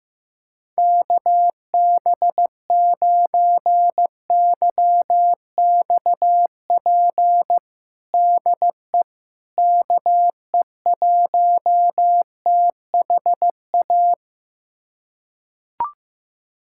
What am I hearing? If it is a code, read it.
Morse code, heard as a keyed tone, about 15 wpm, 700 Hz. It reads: KB9YXP DE KE1THA
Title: USA Callsign DE Callsign 15wpm